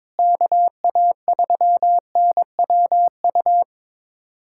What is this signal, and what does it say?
Morse code, heard as a keyed tone, about 21 wpm, 700 Hz. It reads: KA3NWU
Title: USA Callsign DE Callsign - 6 Characters - 22wpm